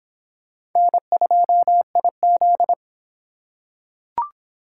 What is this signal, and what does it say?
Morse code, heard as a keyed tone, about 25 wpm, 700 Hz. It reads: N2IZ